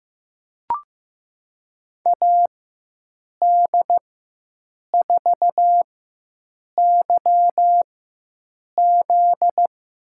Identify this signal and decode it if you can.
Morse code, heard as a keyed tone, about 15 wpm, spaced as for 7 wpm, 700 Hz. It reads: AD4YZ
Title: USA Call Signs - 5 Characters - 4x Character Spacing - 15wpm